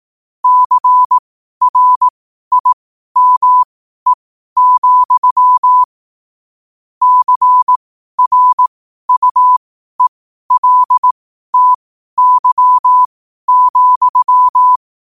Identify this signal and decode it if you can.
Morse code, heard as a keyed tone, about 18 wpm, spaced as for 12 wpm, 1000 Hz. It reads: CRIME, CRUELTY,
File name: qotdfiles / 12wpm